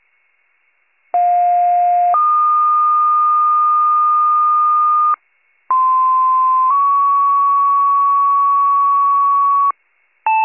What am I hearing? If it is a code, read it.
Two-tone page, heard as a one-second tone, then a three-second tone, about 710 then 1220 Hz, about 1040 then 1120 Hz. It is heard from 1.1 to 9.7 s.